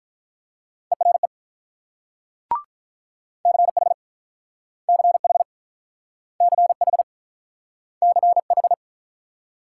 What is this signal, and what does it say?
Morse code, heard as a keyed tone, about 47 wpm, 700 Hz. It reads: ERE CH CH CH CH